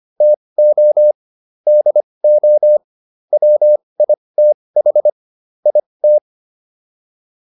Morse code, heard as a keyed tone, about 25 wpm, spaced as for 20 wpm, 600 Hz. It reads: TO DO WITH IT
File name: WD304 20